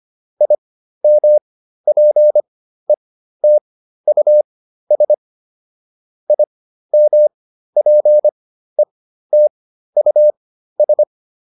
Morse code, heard as a keyed tone, about 25 wpm, spaced as for 13 wpm, 600 Hz. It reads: IMPETUS IMPETUS
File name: IM7 13